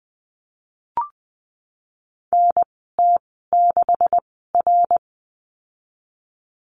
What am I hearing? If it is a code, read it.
Morse code, heard as a keyed tone, about 20 wpm, spaced as for 15 wpm, 700 Hz. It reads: NT6R
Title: 147 Lesson 24g - Letter y - Call Signs 20wpm